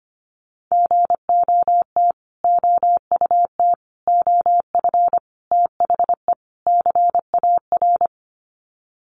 Morse code, heard as a keyed tone, about 25 wpm, 700 Hz. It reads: GOT OUT OF THE CAR
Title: Sets of 5 Words Encore 25wpm